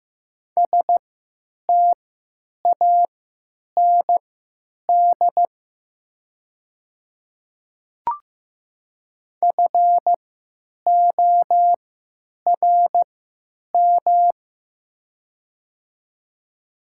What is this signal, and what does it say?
Morse code, heard as a keyed tone, about 16 wpm, spaced as for 9 wpm, 700 Hz. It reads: STAND FORM
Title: 3x Character Spacing - Top 500 Words - 15wpm